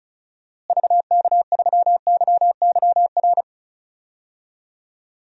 Morse code, heard as a keyed tone, about 34 wpm, 700 Hz. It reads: VK3YYR
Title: DXCC-100 Call Signs - 6 Characters - Repeated 2x - 35wpm